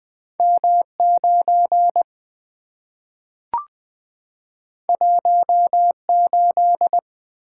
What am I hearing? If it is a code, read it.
Morse code, heard as a keyed tone, about 20 wpm, 700 Hz. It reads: M9 18
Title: Two Letter-Number Combinations - 20wpm